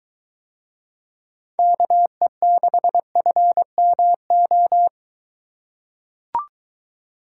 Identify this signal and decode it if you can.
Morse code, heard as a keyed tone, about 23 wpm, 700 Hz. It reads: KE6FMO